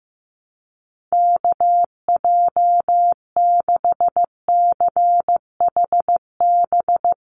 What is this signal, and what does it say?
Morse code, heard as a keyed tone, about 15 wpm, 700 Hz. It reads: KJ6CHB
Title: USA Call Signs 15wpm